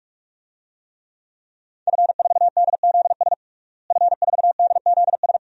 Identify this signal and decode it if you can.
Morse code, heard as a keyed tone, about 46 wpm, 700 Hz. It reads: F4DZS F4DZS
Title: DXCC-100 Call Signs - 5 Characters - Repeated 3x - 45wpm